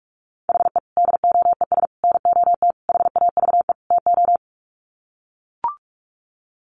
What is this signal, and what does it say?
Morse code, heard as a keyed tone, about 44 wpm, 700 Hz. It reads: HE DOES NOT HAVE TO
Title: Sets of 5 Words Encore 45wpm